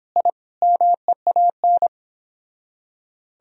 Morse code, heard as a keyed tone, about 29 wpm, 700 Hz. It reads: I MEAN